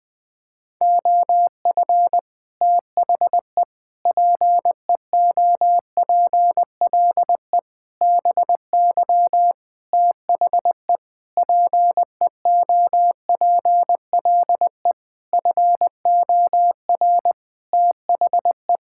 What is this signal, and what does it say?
Morse code, heard as a keyed tone, about 20 wpm, 700 Hz. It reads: OF THE PEOPLE BY THE PEOPLE FOR THE
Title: Sentences with 8-Word Phrases - 20wpm